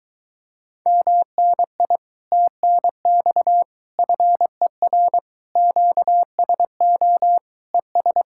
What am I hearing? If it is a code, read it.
Morse code, heard as a keyed tone, about 24 wpm, 700 Hz. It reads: MNI TNX FER QSO ES